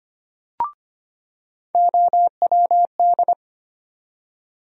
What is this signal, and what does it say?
Morse code, heard as a keyed tone, about 25 wpm, 700 Hz. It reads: OWD
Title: USA Call Sign Postfixes - 3 Letters 25wpm